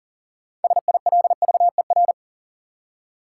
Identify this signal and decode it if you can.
Morse code, heard as a keyed tone, about 39 wpm, 700 Hz. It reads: SILVER
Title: Six Letter Words - 40wpm